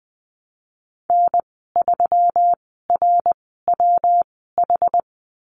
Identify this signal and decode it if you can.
Morse code, heard as a keyed tone, about 20 wpm, spaced as for 15 wpm, 700 Hz. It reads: N3RWH
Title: 167 Lesson 27g - Number 7 - Call Signs 20wpm